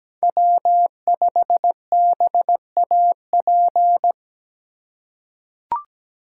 Morse code, heard as a keyed tone, about 17 wpm, 700 Hz. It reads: W5BAP
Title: USA Call Signs - 5 Characters - 2x Character Spacing - 17wpm